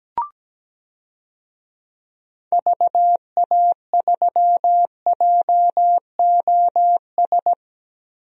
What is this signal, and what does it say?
Morse code, heard as a keyed tone, about 17 wpm, 700 Hz. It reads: VA3JOS
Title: DXCC-100 Call Signs - 6 Characters - 17wpm